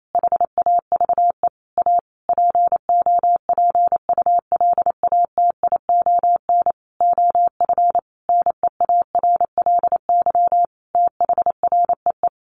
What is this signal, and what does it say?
Morse code, heard as a keyed tone, about 28 wpm, 700 Hz. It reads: HAVE A POPULATION OF NEARLY THREE
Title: Sentences from Top 1300 Words 28wpm